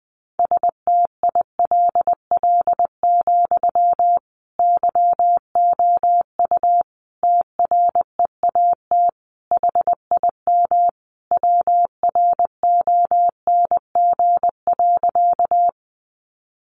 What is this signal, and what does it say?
Morse code, heard as a keyed tone, about 20 wpm, 700 Hz. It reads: STILL, YOU TREAT HIM WRONG.
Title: Sentences from Top 1300 Words 20wpm